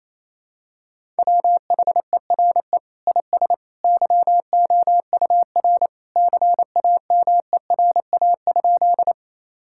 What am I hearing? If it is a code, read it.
Morse code, heard as a keyed tone, about 28 wpm, 700 Hz. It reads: WHERE IS YOUR CAMERA?
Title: Sentences from Top 1000 Words - Review Part 5 - 28wpm